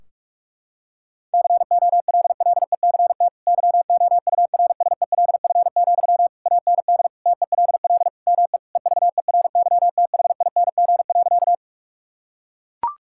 Morse code, heard as a keyed tone, about 41 wpm, 700 Hz. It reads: COLLECT YOURSELF, AND TELL ME EVERYTHING.
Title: Sentences from Top 1600 Words 45wpm